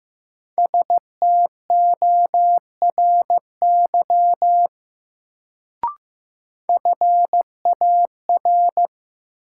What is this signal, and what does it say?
Morse code, heard as a keyed tone, about 15 wpm, 700 Hz. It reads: STORY FAR
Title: Sentences from Top 300 Words - New Word Drill - 15wpm